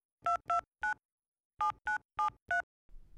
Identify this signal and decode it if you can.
DTMF, heard as telephone touch tones, tones of about 120 ms. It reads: 229 7973